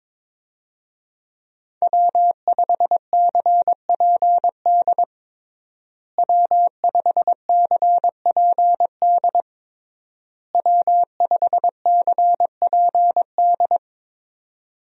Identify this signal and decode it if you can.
Morse code, heard as a keyed tone, about 21 wpm, 700 Hz. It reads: W5CPD W5CPD W5CPD
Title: USA Call Signs - 5 Characters - Repeated 3x - 22wpm